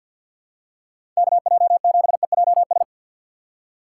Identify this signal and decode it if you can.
Morse code, heard as a keyed tone, about 53 wpm, 700 Hz. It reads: KJ7EJS
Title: USA Call Signs - 6 Characters - 4x Character Spacing - 50wpm